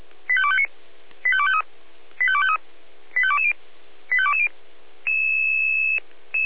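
Five-tone selective call (ZVEI), heard as quick runs of five tones, tones of about 70 ms. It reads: 85259 85253 85253 85220 85220
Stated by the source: ZVEI five-tone selective call received by SDR.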